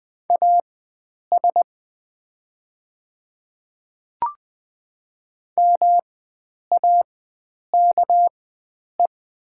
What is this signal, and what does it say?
Morse code, heard as a keyed tone, about 20 wpm, spaced as for 9 wpm, 700 Hz. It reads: AS MAKE
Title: 4x Character Spacing - Top 100 Words - 20wpm